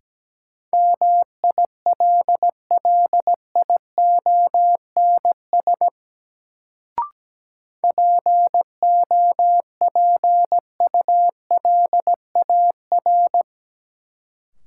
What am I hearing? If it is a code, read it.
Morse code, heard as a keyed tone, about 17 wpm, 700 Hz. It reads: MILLIONS POPULAR
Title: Sets of 5 Words - New Word Drill - 17wpm